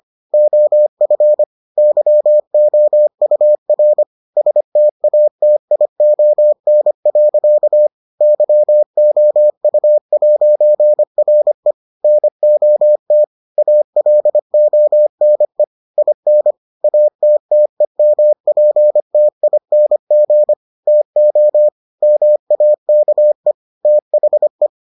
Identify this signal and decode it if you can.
Morse code, heard as a keyed tone, about 25 wpm, 600 Hz. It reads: OF YOUR STATION. YOU'RE NOT ALONE IN ATTEMPTING TO MAKE THE